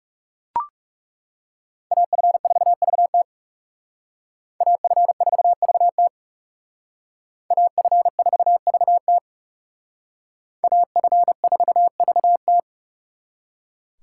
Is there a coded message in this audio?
Yes: Morse code from 1.9 to 12.6 s, heard as a keyed tone, about 38 wpm, 700 Hz. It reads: AF4VT AF4VT AF4VT AF4VT